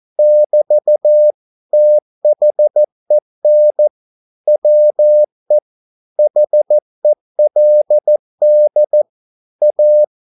Morse code, heard as a keyed tone, about 14 wpm, 600 Hz. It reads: <BT> THEN WE HELD A
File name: mp3 files / a dogs mistake - 14wpm